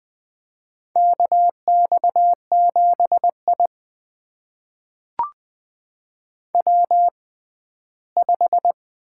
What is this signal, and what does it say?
Morse code, heard as a keyed tone, about 20 wpm, spaced as for 14 wpm, 700 Hz. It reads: KX7I W5
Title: USA Call Signs - 4 Characters - 6x Character Spacing - 20wpm